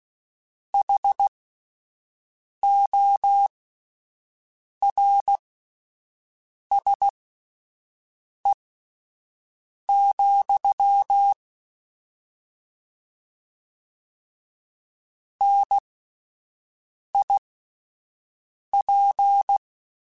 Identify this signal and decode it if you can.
Morse code, heard as a keyed tone, about 16 wpm, spaced as for 5 wpm, 800 Hz. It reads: HORSE, NIP
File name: Misc / Misc-mp3s / Word Pairs @5